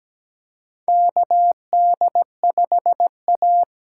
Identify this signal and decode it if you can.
Morse code, heard as a keyed tone, about 17 wpm, 700 Hz. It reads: KD5A
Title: USA Call Signs - 6 Characters - 8x Character Spacing - 17wpm